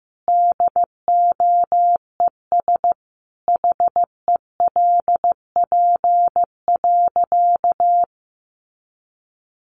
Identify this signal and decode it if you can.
Morse code, heard as a keyed tone, about 15 wpm, 700 Hz. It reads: DOES HELP.
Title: Sentences from Top 700 Words 15wpm